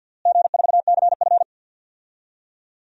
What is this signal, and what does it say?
Morse code, heard as a keyed tone, about 53 wpm, 700 Hz. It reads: G4CF
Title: DXCC-100 Call Signs - 4 Characters - Repeated 3x - 50wpm